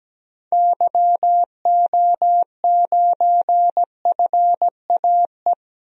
Morse code, heard as a keyed tone, about 17 wpm, 700 Hz. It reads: YO9FAE
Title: DXCC-100 Call Signs - 6 Characters - 17wpm